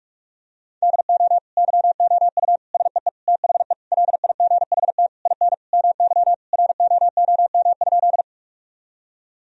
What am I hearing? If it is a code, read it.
Morse code, heard as a keyed tone, about 41 wpm, 700 Hz. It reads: DO YOU SEE THE LIGHT IN MY ROOM?
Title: Sentences from Top 1000 Words - Review Part 1 - 45wpm